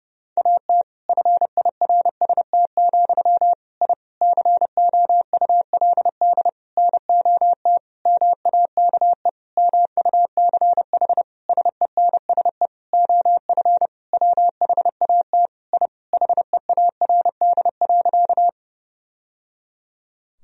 Morse code, heard as a keyed tone, about 32 wpm, 700 Hz. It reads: AT FIRST, I COULD NOT MAKE MUCH SENSE OF WHAT I HEARD.
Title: Sentences from Top 1000 Words - Review Part 4 - 30wpm